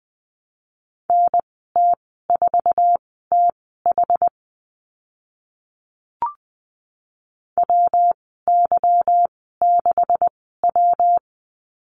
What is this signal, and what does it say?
Morse code, heard as a keyed tone, about 20 wpm, spaced as for 14 wpm, 700 Hz. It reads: NT4TH WY6W
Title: 154 Lesson 25g - Letter p - Call Signs 20wpm